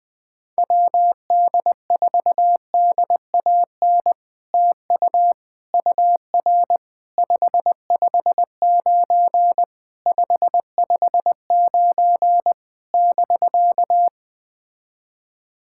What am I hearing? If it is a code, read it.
Morse code, heard as a keyed tone, about 20 wpm, 700 Hz. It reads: WD4DAN TU UR 559 559 <BK>
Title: POTA QSO Practice - 20wpm